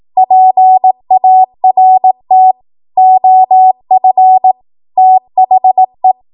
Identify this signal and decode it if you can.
Morse code, heard as a keyed tone, about 18 wpm, 750 Hz. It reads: PART OF THE
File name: Morse / 18WPM